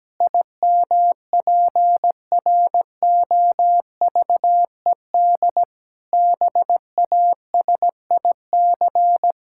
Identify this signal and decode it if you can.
Morse code, heard as a keyed tone, about 17 wpm, 700 Hz. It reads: IMPROVED BASIC
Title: Pangram Challenge v2 - 17wpm